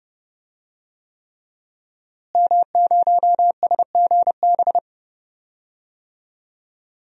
Morse code, heard as a keyed tone, about 30 wpm, 700 Hz. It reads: M0SGB
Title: DXCC-100 Call Signs - 5 Characters - Repeated 3x - 30wpm